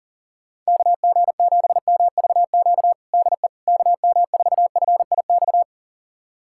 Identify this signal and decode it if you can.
Morse code, heard as a keyed tone, about 40 wpm, 700 Hz. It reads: KG7MVQ DE KM4FIX
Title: USA Callsign DE Callsign 40wpm